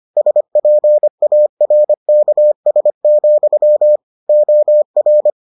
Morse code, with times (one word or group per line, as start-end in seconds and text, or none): 0.2-4.0 SPARKS,
4.3-5.3 OR